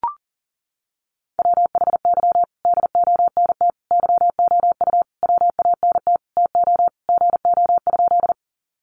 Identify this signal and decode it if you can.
Morse code, heard as a keyed tone, about 40 wpm, 700 Hz. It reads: WHY DONT YOU WANT TO GO?
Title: Sentences from Top 300 Words 40wpm